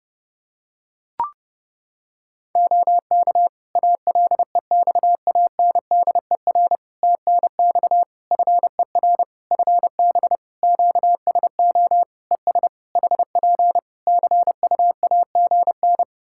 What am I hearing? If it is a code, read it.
Morse code, heard as a keyed tone, about 30 wpm, 700 Hz. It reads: OK ALEXANDER TNX FER FB QSO ES HP CUAGN
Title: K1USN SST QSO Practice 30wpm